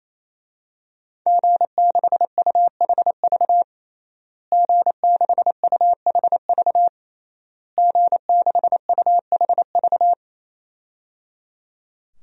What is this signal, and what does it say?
Morse code, heard as a keyed tone, about 28 wpm, 700 Hz. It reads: G6UHV G6UHV G6UHV
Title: DXCC-100 Call Signs - 5 Characters - Repeated 3x - 28wpm